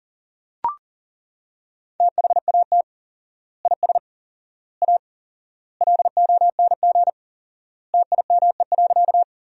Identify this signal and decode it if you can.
Morse code, heard as a keyed tone, about 40 wpm, 700 Hz. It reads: THAT IS A LONG TIME.
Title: 4x Word-Spacing - Sentences from Top 100 Words - 40wpm